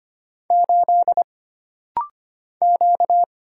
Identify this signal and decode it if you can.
Morse code, heard as a keyed tone, about 25 wpm, 700 Hz. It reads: 8 Q
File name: Single Character (Letter:Number:Punctuation:Prosign) - Rapid-Fire - 25wpm